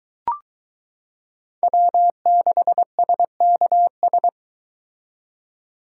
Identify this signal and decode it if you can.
Morse code, heard as a keyed tone, about 22 wpm, 700 Hz. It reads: W6SKS